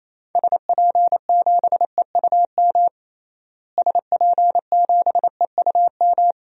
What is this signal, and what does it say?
Morse code, heard as a keyed tone, about 28 wpm, 700 Hz. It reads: SP7EUM SP7EUM
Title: DXCC-100 Call Signs - 6 Characters - Repeated 2x - 28wpm